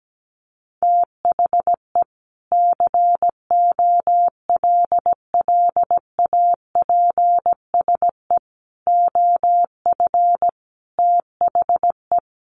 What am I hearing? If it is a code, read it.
Morse code, heard as a keyed tone, about 17 wpm, 700 Hz. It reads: THE COLLAPSE OF THE
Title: Sets of 5 Words Encore 17wpm